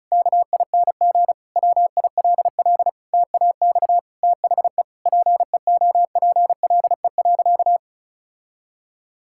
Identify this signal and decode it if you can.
Morse code, heard as a keyed tone, about 36 wpm, 700 Hz. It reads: KING WILL TAX THE PEOPLE.
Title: Sentences from Top 1000 Words - Review Part 6 - 35wpm